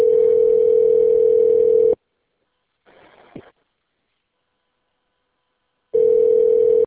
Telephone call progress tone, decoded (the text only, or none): ringback tone